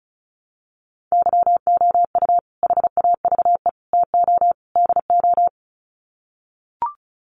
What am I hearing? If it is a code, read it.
Morse code, heard as a keyed tone, about 34 wpm, 700 Hz. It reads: YOU HAVE TO DO
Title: Sets of 4 Words 35wpm